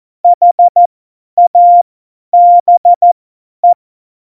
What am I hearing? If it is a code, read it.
Morse code, heard as a keyed tone, about 14 wpm, spaced as for 10 wpm, 700 Hz. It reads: HABE
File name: Morsekurs / Witze / Witz 17 April 2018, 50 F70